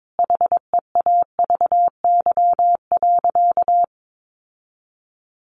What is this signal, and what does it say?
Morse code, heard as a keyed tone, about 22 wpm, 700 Hz. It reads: HEAVY.
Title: Sentences from Top 1200 Words 22wpm